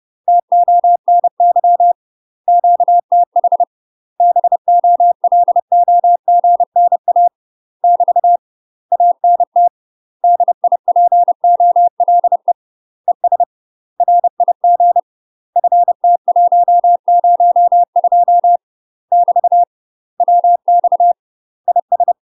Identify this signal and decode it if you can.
Morse code, heard as a keyed tone, about 30 wpm, 700 Hz. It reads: TONY QTH BOLOGNA <BT> ANT DIPOLE ES RIG FT102 <BT> WX IS
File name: UK MASTER QSOs TRAINING / 30WPM / UK MASTER QSOs 0048